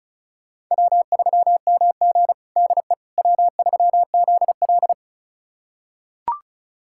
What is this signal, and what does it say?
Morse code, heard as a keyed tone, about 34 wpm, 700 Hz. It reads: W3MG DE W3ZL
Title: USA Callsign DE Callsign - 4 Characters - 35wpm